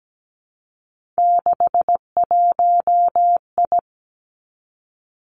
Morse code, heard as a keyed tone, about 17 wpm, 700 Hz. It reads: 61I
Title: Three Challenging Letter-Number Combinations 17wpm